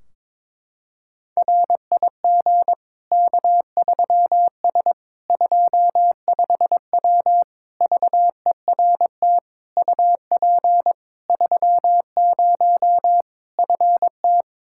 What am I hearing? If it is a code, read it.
Morse code, heard as a keyed tone, about 22 wpm, 700 Hz. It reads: RIG K3S 25W VERT UP 30 FT
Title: Practice QSO - Exchange 2 - Info - 22wpm